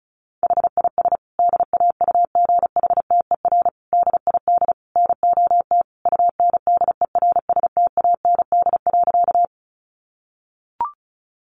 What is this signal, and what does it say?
Morse code, heard as a keyed tone, about 34 wpm, 700 Hz. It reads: HIS DAUGHTER DID NOT UNDERSTAND.
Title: Sentences from Top 1500 Words 35wpm